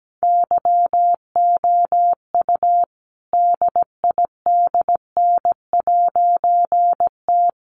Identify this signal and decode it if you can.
Morse code, heard as a keyed tone, about 17 wpm, 700 Hz. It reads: YOU DIDN'T
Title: A Sound Of Thunder 17wpm